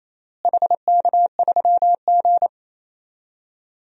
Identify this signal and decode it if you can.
Morse code, heard as a keyed tone, about 27 wpm, 700 Hz. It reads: HK3G
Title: DXCC-100 Call Signs - 4 Characters - Repeated 3x - 28wpm